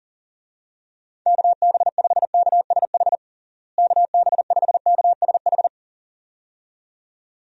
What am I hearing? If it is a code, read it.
Morse code, heard as a keyed tone, about 40 wpm, 700 Hz. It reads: KB5KSH KB5KSH
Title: USA Call Signs - 6 Characters - Repeated 2x - 40wpm